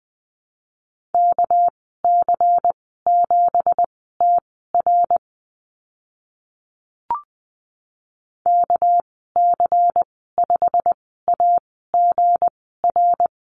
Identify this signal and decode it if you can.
Morse code, heard as a keyed tone, about 20 wpm, spaced as for 14 wpm, 700 Hz. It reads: KC7TR KC5AGR
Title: 213 Lesson 34g - Number 8 - Call Signs 20wpm